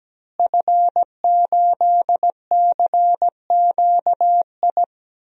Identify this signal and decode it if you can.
Morse code, heard as a keyed tone, about 17 wpm, 700 Hz. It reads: F8CQI
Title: DXCC-100 Call Signs - 5 Characters - Repeated 3x - 17wpm